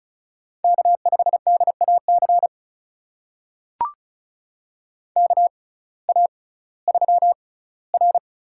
Morse code, heard as a keyed tone, about 36 wpm, spaced as for 19 wpm, 700 Hz. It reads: K5DAC KA3R
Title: USA Call Signs - 6x Character Spacing - 35wpm